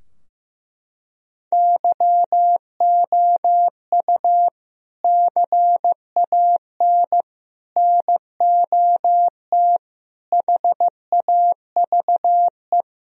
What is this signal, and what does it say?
Morse code, heard as a keyed tone, about 15 wpm, 700 Hz. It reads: YOU CAN NOT HAVE
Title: Sentences from Top 1000 Words - Review Part 4 - 15wpm